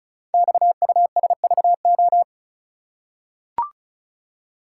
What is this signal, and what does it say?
Morse code, heard as a keyed tone, about 36 wpm, 700 Hz. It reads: XUSVO
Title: Code Groups - 5 Letter:Numbers - 35wpm